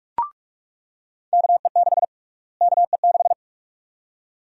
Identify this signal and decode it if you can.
Morse code, heard as a keyed tone, about 42 wpm, 700 Hz. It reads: KE6 KE6
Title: USA Call Sign Prefixes - Repeated 2x - 45wpm